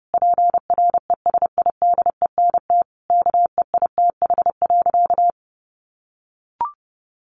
Morse code, heard as a keyed tone, about 30 wpm, 700 Hz. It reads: PRESIDENT KEITH.
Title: A Sound Of Thunder 30wpm